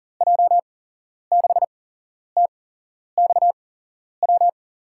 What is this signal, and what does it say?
Morse code, heard as a keyed tone, about 40 wpm, spaced as for 11 wpm, 700 Hz. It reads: J6TXW
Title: USA Call Signs - 8x Character Spacing - 40wpm